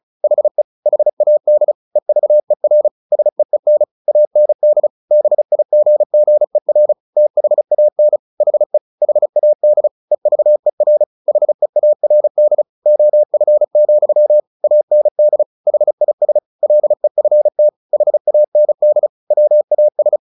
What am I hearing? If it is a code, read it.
Morse code, heard as a keyed tone, about 34 wpm, 600 Hz. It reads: HE HAD EVER SEEN AND BIGGER THAN HE HAD EVER HEARD OF, AND HIS LEFT HAND WAS
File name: the old man and the sea / the old man and the sea 1 35wpm